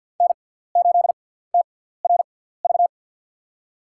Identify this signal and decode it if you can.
Morse code, heard as a keyed tone, about 52 wpm, spaced as for 17 wpm, 690 Hz. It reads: N8TRV
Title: USA Call Signs - 6x Character Spacing - 50wpm